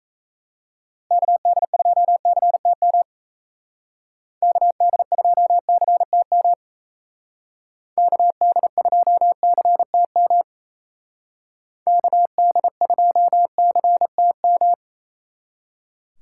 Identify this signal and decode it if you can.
Morse code, heard as a keyed tone, about 33 wpm, 700 Hz. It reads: KD2CTM KD2CTM KD2CTM KD2CTM
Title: Speed Racing - USA 2x3 Call Signs 28wpm